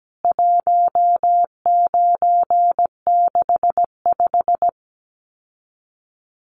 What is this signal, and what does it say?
Morse code, heard as a keyed tone, about 17 wpm, 700 Hz. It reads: 1965
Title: CWT Exchange 17wpm